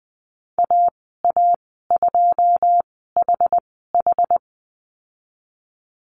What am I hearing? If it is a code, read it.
Morse code, heard as a keyed tone, about 20 wpm, spaced as for 15 wpm, 700 Hz. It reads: AA2HH
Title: 082 Lesson 14g - Number 2 - Call Signs 20wpm